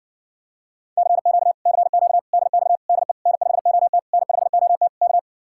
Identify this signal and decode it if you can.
Morse code, heard as a keyed tone, about 60 wpm, 700 Hz. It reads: CQ CQ DX DE N5OT N5OT K